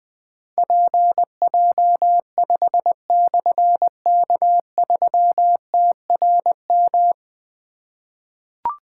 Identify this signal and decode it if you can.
Morse code, heard as a keyed tone, about 20 wpm, 700 Hz. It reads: PJ5/K3TRM
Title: Difficult Call Signs 20wpm